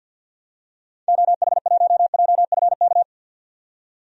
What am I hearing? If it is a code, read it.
Morse code, heard as a keyed tone, about 52 wpm, 700 Hz. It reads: OH1JFK
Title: DXCC-100 Call Signs - 6 Characters - 50wpm